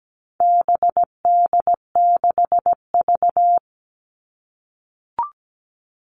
Morse code, heard as a keyed tone, about 17 wpm, 700 Hz. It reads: BD6V
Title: All Cut Numbers - 4-Digit 17wpm